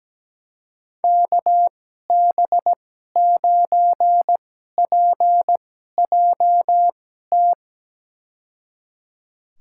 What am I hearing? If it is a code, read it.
Morse code, heard as a keyed tone, about 17 wpm, spaced as for 12 wpm, 700 Hz. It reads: KB9PJT
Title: USA Call Signs - 6 Characters - 2x Character Spacing - 17wpm